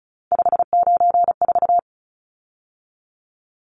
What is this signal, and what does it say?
Morse code, heard as a keyed tone, about 34 wpm, 700 Hz. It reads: 594
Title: Numbers - 3-Digit 35wpm